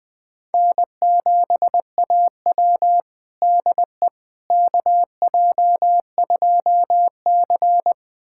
Morse code, heard as a keyed tone, about 20 wpm, 700 Hz. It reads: N7AW DE KJ2C
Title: USA Callsign DE Callsign - 4 Characters - 20wpm